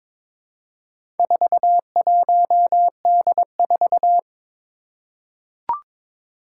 Morse code, heard as a keyed tone, about 22 wpm, 700 Hz. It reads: 41D4
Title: Four Challenging Letter-Number Combinations 22wpm